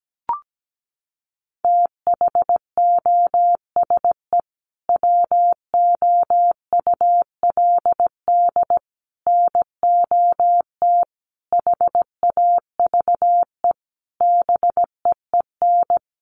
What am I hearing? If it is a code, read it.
Morse code, heard as a keyed tone, about 17 wpm, 700 Hz. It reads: THOSE WOULD NOT HAVE BEEN
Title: Sentences from Top 1300 Words 17wpm